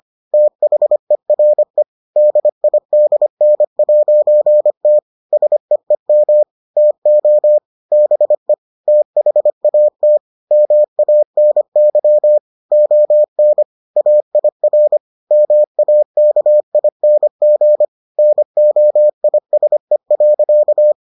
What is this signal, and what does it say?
Morse code, heard as a keyed tone, about 25 wpm, 600 Hz. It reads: THERE DIDN'T SEEM TO BE THAT MANY ON AIR MAKING NOISE.